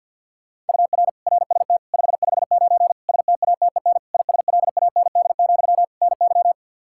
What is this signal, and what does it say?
Morse code, heard as a keyed tone, about 52 wpm, 700 Hz. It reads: UR RST 559 STATEN ISLAND, NY
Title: Practice QSO - Exchange 1 - Beginning - 50wpm